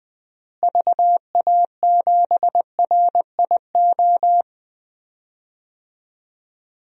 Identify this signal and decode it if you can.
Morse code, heard as a keyed tone, about 21 wpm, 700 Hz. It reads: VA7RIO
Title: DXCC-100 Call Signs - 6 Characters - Repeated 2x - 20wpm